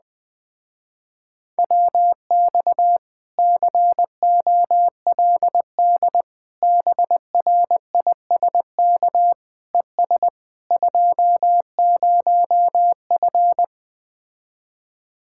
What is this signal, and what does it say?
Morse code, heard as a keyed tone, about 20 wpm, 700 Hz. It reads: WX COLD BRISK ES 20F
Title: Speed-Racing Common QSO Phrases - 17wpm